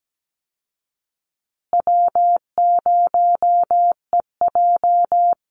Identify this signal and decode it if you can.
Morse code, heard as a keyed tone, about 17 wpm, 700 Hz. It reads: W0EJ
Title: Easy Call Signs Repeated 3x - 17wpm